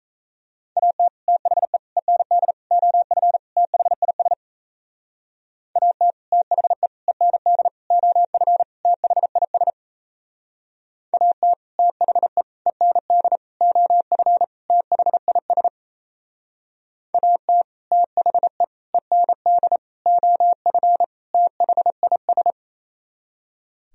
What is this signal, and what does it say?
Morse code, heard as a keyed tone, about 37 wpm, 700 Hz. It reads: AT THE END OF THIS AT THE END OF THIS AT THE END OF THIS AT THE END OF THIS